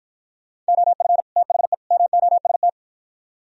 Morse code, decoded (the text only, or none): OF THE MOST